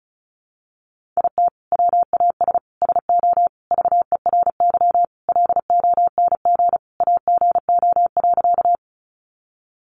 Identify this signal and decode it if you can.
Morse code, heard as a keyed tone, about 34 wpm, 700 Hz. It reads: IT WAS SO VERY LONG AGO.
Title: Sentences from Top 500 Words 35wpm